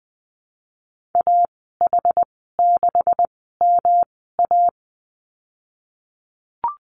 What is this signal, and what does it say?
Morse code, heard as a keyed tone, about 20 wpm, spaced as for 15 wpm, 700 Hz. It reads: AH6MA
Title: 167 Lesson 27g - Number 7 - Call Signs 20wpm